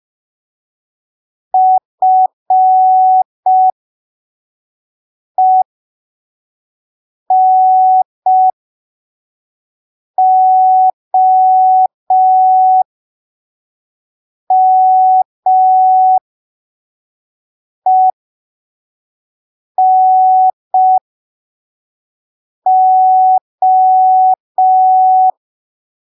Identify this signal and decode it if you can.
Morse code, heard as a keyed tone, about 5 wpm, spaced as for 3 wpm, 750 Hz. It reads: FENOMENO
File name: arquivos / texto / texto 24 05ppm